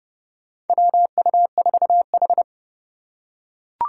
Morse code, heard as a keyed tone, about 31 wpm, 700 Hz. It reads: WU4H
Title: USA Call Signs - 3x Character Spacing - 30wpm